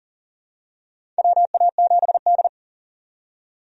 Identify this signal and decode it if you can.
Morse code, heard as a keyed tone, about 40 wpm, 700 Hz. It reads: WA7D
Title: USA Call Signs - 4 Characters - 40wpm